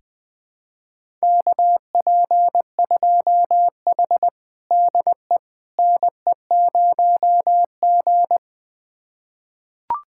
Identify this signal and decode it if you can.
Morse code, heard as a keyed tone, about 20 wpm, 700 Hz. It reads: KP2H DE NE0G
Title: USA Callsign DE Callsign - 4 Characters - 20wpm